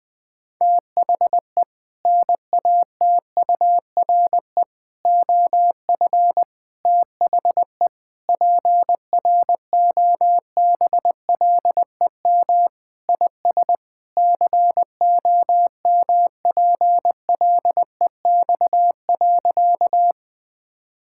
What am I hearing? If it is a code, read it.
Morse code, heard as a keyed tone, about 20 wpm, 700 Hz. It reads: THE NATURE OF THE PROBLEM IS COMPLEX.
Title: Sentences with 4-Word Phrases - 20wpm